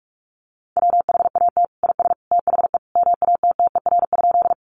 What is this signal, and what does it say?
Morse code, heard as a keyed tone, about 44 wpm, 700 Hz. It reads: WHAT IS THE MATTER?
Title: Sentences from Top 700 Words 45wpm